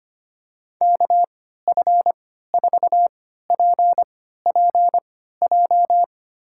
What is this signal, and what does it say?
Morse code, heard as a keyed tone, about 25 wpm, spaced as for 14 wpm, 700 Hz. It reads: KF4PPJ
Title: USA Call Signs - 6 Characters - 3x Character Spacing - 25wpm